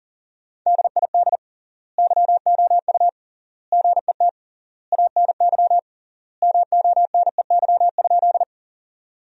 Morse code, heard as a keyed tone, about 40 wpm, 700 Hz. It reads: DID YOU GET ANY MONEY?
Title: 3x Word-Spacing - Sentences from Top 400 Words - 40wpm